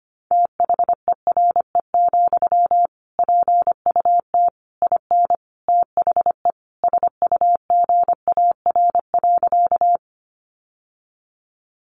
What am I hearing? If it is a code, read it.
Morse code, heard as a keyed tone, about 25 wpm, 700 Hz. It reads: THERE, PUT IN THE SUGAR.
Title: Sentences from Top 1200 Words 25wpm